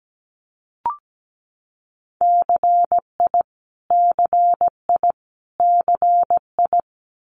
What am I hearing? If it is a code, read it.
Morse code, heard as a keyed tone, about 17 wpm, 700 Hz. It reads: CI CI CI
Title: Multi-letter Phonemes Repeated 3x 17wpm